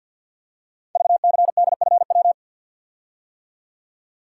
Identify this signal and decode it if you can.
Morse code, heard as a keyed tone, about 46 wpm, 700 Hz. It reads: VCDFW
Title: Code Groups - 5 Letters - 50wpm